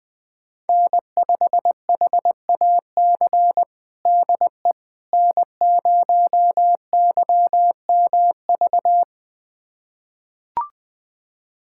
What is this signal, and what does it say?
Morse code, heard as a keyed tone, about 21 wpm, 700 Hz. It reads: N5HAC DE N0YMV
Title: USA Callsign DE Callsign - 5 Characters - 20wpm